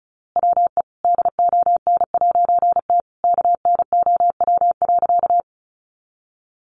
Morse code, heard as a keyed tone, about 34 wpm, 700 Hz. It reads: WE DON'T KNOW.
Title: A Sound Of Thunder 35wpm